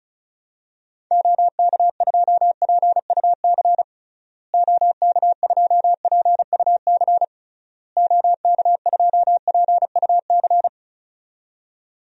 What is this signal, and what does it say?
Morse code, heard as a keyed tone, about 34 wpm, 700 Hz. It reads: OK2PUC OK2PUC OK2PUC
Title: DXCC-100 Call Signs - 6 Characters - Repeated 3x - 35wpm